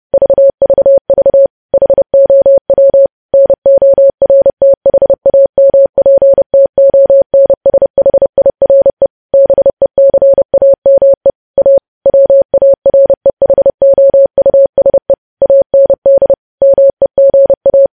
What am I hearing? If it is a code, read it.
Morse code, heard as a keyed tone, about 30 wpm, 550 Hz. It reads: VVV HOW NORTHAMPTONSHIRE BECAME A WAREHOUSE AND MEGA
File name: podcast / News headlines-1x-30x30